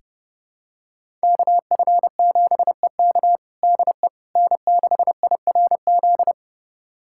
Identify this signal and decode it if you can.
Morse code, heard as a keyed tone, about 31 wpm, 700 Hz. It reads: KF7EK DE N6IRZ
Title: USA Callsign DE Callsign - 5 Characters - 30wpm